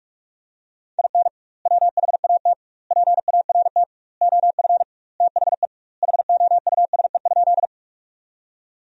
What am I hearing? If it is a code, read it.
Morse code, heard as a keyed tone, about 46 wpm, 700 Hz. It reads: IN WHAT PART OF THE HOUSE?
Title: 2x Word-Spacing - Sentences from Top 200 Words - 45wpm